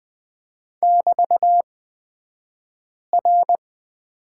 Morse code, heard as a keyed tone, about 20 wpm, spaced as for 5 wpm, 700 Hz. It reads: <BT>R
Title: 223 Lesson 36d - BT - Two Characters 20wpm